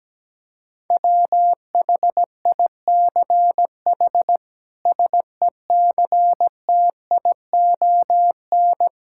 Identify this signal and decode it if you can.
Morse code, heard as a keyed tone, about 17 wpm, 700 Hz. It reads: WHICH SECTION
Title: Sentences from Top 1000 Words - Review Part 5 - 17wpm